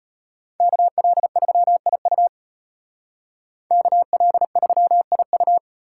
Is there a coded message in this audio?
Yes: Morse code from 0.6 to 5.6 s, heard as a keyed tone, about 33 wpm, 700 Hz. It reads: KL3IU KL3IU